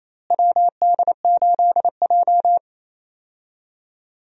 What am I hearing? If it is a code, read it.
Morse code, heard as a keyed tone, about 28 wpm, 700 Hz. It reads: WD8J
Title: USA Call Signs - 4 Characters - Repeated 3x - 28wpm